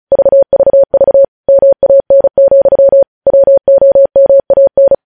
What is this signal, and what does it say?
Morse code, heard as a keyed tone, about 36 wpm, 550 Hz. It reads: VVV MAN, WOMAN
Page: News headlines in Morse code at 35 wpm